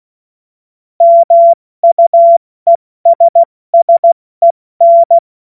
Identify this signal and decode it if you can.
Morse code, heard as a keyed tone, about 16 wpm, 680 Hz. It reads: MUESSEN
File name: Morsekurs / Witze / Witz 19 Dezember 2023 70 F80